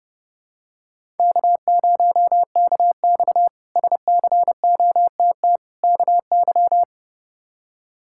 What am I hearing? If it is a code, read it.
Morse code, heard as a keyed tone, about 30 wpm, 700 Hz. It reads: K0KX SCOTT KY
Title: K1USN SST QSO Practice - 30wpm